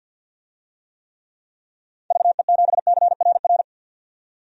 Morse code, heard as a keyed tone, about 52 wpm, 700 Hz. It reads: VE7CRR
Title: DXCC-100 Call Signs - 6 Characters - 50wpm